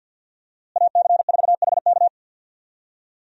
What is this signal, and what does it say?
Morse code, heard as a keyed tone, about 52 wpm, 700 Hz. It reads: AC4HK
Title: USA Call Signs - 5 Characters - Repeated 2x - 50wpm